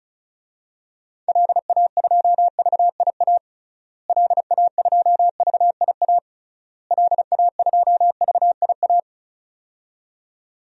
Morse code, heard as a keyed tone, about 34 wpm, 700 Hz. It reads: LA2VIA LA2VIA LA2VIA
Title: DXCC-100 Call Signs - 6 Characters - Repeated 3x - 35wpm